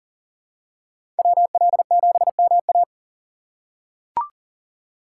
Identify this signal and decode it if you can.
Morse code, heard as a keyed tone, about 40 wpm, 700 Hz. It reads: WL7MA
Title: USA Call Signs - 5 Characters - 4x Character Spacing - 40wpm